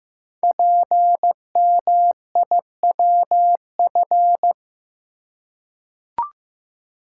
Morse code, heard as a keyed tone, about 15 wpm, 700 Hz. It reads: PMIWF
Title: Code Groups - 5 Letter:Numbers - 15wpm